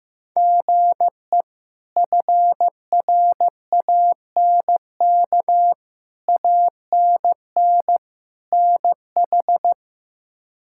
Morse code, heard as a keyed tone, about 15 wpm, 700 Hz. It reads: GE FRANK ANN NH
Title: K1USN SST QSO Practice - 15wpm